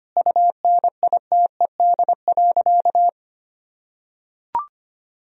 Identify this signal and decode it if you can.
Morse code, heard as a keyed tone, about 25 wpm, 700 Hz. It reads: UNITED.
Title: Sentences from Top 1000 Words - Review Part 4 - 25wpm